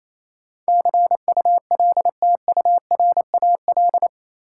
Morse code, heard as a keyed tone, about 28 wpm, 700 Hz. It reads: CULTURAL
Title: Sentences from Top 1500 Words - New Word Drill - 28wpm